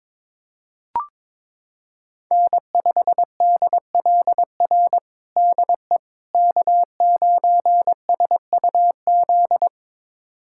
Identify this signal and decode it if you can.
Morse code, heard as a keyed tone, about 21 wpm, 700 Hz. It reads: N5DLR DE K9SUZ
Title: USA Callsign DE Callsign - 5 Characters - 22wpm